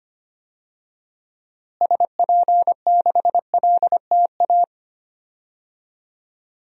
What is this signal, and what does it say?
Morse code, heard as a keyed tone, about 25 wpm, 700 Hz. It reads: SP6LTA
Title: DXCC-100 Call Signs - 6 Characters - Repeated 3x - 25wpm